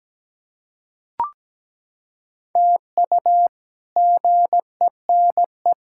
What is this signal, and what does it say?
Morse code, heard as a keyed tone, about 17 wpm, 700 Hz. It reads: TU GENE
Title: K1USN SST QSO Practice - 17wpm